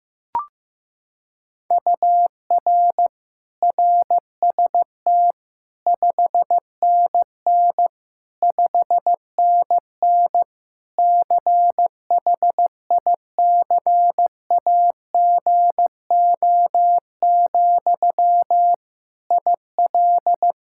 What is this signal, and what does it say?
Morse code, heard as a keyed tone, about 15 wpm, 700 Hz. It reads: UR RST 5NN 5NN CHICAGO, IL
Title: K1USN SST QSO Practice 15wpm